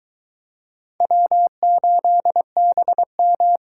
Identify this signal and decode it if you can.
Morse code, heard as a keyed tone, about 22 wpm, 700 Hz. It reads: W8BM